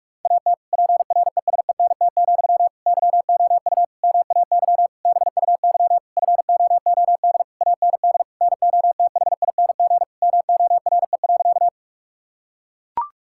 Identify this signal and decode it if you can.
Morse code, heard as a keyed tone, about 42 wpm, 700 Hz. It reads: AT PRESENT, YOU MAY BUY FOOD AND NOTHING MORE.
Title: Sentences from Top 1000 Words - Review Part 5 - 45wpm